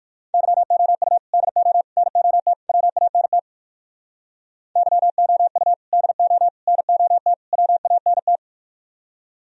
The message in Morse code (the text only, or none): YOU DO NOT WANT YOU DO NOT WANT